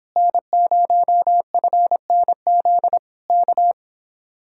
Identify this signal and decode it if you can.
Morse code, heard as a keyed tone, about 26 wpm, 700 Hz. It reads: N0FNZ K